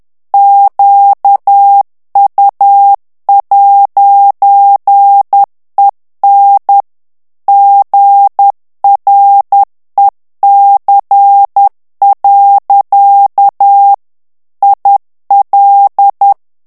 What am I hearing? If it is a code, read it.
Morse code, heard as a keyed tone, about 10 wpm, 800 Hz. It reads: QU'EN GREC. IL
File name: FRANCAIS TEXTE CLAIR 2 VITESSE 10 MOTS MINUTES